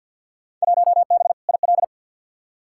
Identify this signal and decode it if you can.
Morse code, heard as a keyed tone, about 46 wpm, 700 Hz. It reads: 1B IL
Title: Field Day Exchanges - 50wpm